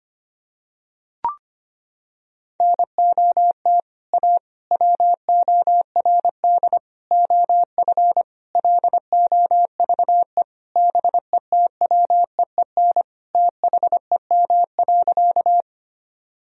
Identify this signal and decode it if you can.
Morse code, heard as a keyed tone, about 25 wpm, 700 Hz. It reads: NOT A WORD OF LOVE BETWEEN THEM.
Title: Sentences from Top 1000 Words - Review Part 1 - 25wpm